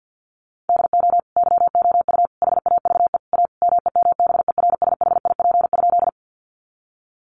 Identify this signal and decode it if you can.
Morse code, heard as a keyed tone, about 50 wpm, 700 Hz. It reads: DO YOU HAVE A MEMBERSHIP?
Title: Sentences from Top 1400 Words 50wpm